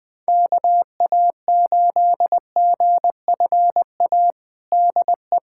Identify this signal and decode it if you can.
Morse code, heard as a keyed tone, about 20 wpm, 700 Hz. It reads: KA8GFA DE
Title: USA Callsign DE Callsign 20wpm